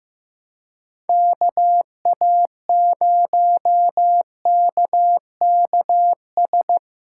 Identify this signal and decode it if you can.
Morse code, heard as a keyed tone, about 15 wpm, 700 Hz. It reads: KA0KKS
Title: USA Call Signs - 6 Characters - 4x Character Spacing - 15wpm